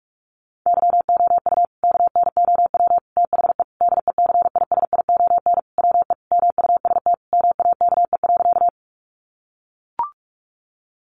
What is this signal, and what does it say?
Morse code, heard as a keyed tone, about 44 wpm, 700 Hz. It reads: YOU KNOW THE DECISION WE MUST MAKE.
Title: Sentences from Top 1000 Words 45wpm